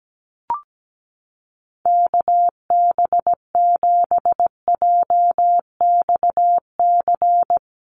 Morse code, heard as a keyed tone, about 17 wpm, 700 Hz. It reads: KB7JXC
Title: USA Call Signs Repeated 3x 17wpm